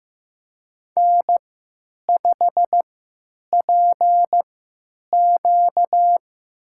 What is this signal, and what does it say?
Morse code, heard as a keyed tone, about 15 wpm, spaced as for 9 wpm, 700 Hz. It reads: N5PQ
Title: USA Call Signs - 3x Character Spacing - 15wpm